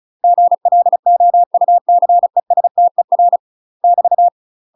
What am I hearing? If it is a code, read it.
Morse code, heard as a keyed tone, about 34 wpm, 700 Hz. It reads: GLOUCESTER <BT>
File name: UK MASTER QSOs TRAINING/35WPM